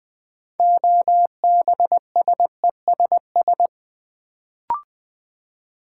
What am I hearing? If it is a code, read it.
Morse code, heard as a keyed tone, about 20 wpm, 690 Hz. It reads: OBSESS
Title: Build A Word - 2 Letters At A Time - 6-Letter Words - 20wpm